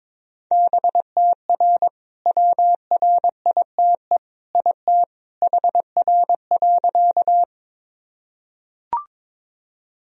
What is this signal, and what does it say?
Morse code, heard as a keyed tone, about 22 wpm, 700 Hz. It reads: BTR WRITE IT HR.
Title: Sentences from Top 400 Words and Abbreviations 22wpm